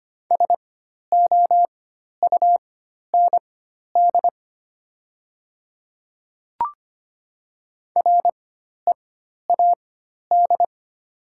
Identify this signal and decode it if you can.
Morse code, heard as a keyed tone, about 25 wpm, spaced as for 12 wpm, 700 Hz. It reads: SOUND READ
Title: 4x Character Spacing - Top 200 Words - 25wpm